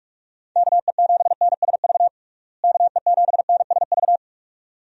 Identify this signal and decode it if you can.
Morse code, heard as a keyed tone, about 46 wpm, 700 Hz. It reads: KE7NSV KE7NSV
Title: USA Call Signs - 6 Characters - Repeated 2x - 45wpm